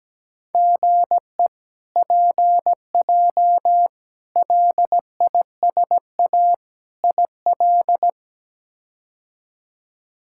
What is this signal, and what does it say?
Morse code, heard as a keyed tone, about 17 wpm, 700 Hz. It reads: GE PJ LISA IL
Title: K1USN SST QSO Practice - 17wpm